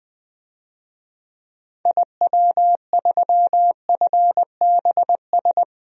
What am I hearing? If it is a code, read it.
Morse code, heard as a keyed tone, about 20 wpm, 700 Hz. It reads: IW3FBS
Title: DXCC-100 Call Signs - 6 Characters - Repeated 2x - 20wpm